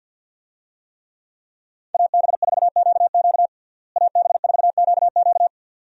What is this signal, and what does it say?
Morse code, heard as a keyed tone, about 52 wpm, 700 Hz. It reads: AB4QQ AB4QQ
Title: USA Call Signs - 5 Characters - Repeated 2x - 50wpm